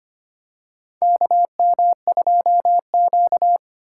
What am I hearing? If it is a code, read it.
Morse code, heard as a keyed tone, about 25 wpm, 700 Hz. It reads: KM2Q
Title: USA Call Signs - 6x Character Spacing - 25wpm